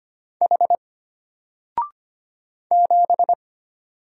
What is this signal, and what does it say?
Morse code, heard as a keyed tone, about 25 wpm, 700 Hz. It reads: H 7
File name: Single-Letter-Number-25wpm